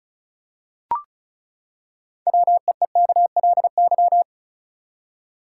Morse code, heard as a keyed tone, about 36 wpm, 700 Hz. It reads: WEEKLY
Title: Six Letter Words - 35wpm